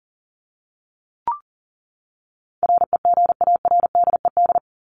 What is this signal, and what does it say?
Morse code, heard as a keyed tone, about 40 wpm, 700 Hz. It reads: REGARDED
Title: ED Suffix - Top 100 Words with Base Words 40wpm